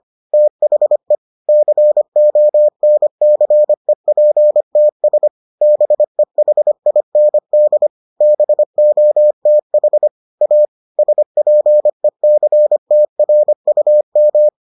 Morse code, heard as a keyed tone, about 25 wpm, 600 Hz. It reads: THE CONCEPTS BEHIND BOTH A SPECTRUM